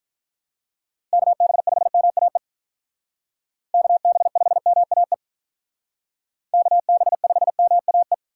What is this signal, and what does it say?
Morse code, heard as a keyed tone, about 46 wpm, 700 Hz. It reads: KB5MAE KB5MAE KB5MAE